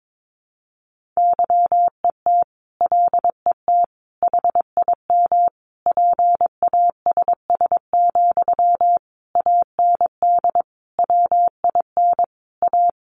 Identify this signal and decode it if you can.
Morse code, heard as a keyed tone, about 22 wpm, 700 Hz. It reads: YET LET HIM PASS, AND WIN A
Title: Sentences from Top 1500 Words 22wpm